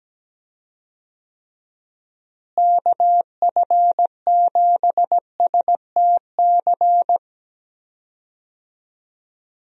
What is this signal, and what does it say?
Morse code, heard as a keyed tone, about 17 wpm, 700 Hz. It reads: KF7STC
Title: USA Call Signs - 6 Characters - Repeated 3x - 17wpm